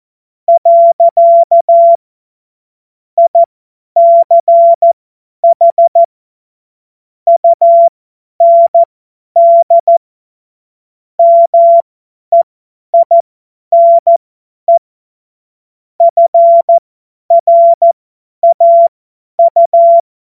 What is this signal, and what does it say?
Morse code, heard as a keyed tone, about 14 wpm, spaced as for 10 wpm, 680 Hz. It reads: . ICH UND MEINE FRAU
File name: Morsekurs / Witze / Witz 06. Juli 2021, 50 F70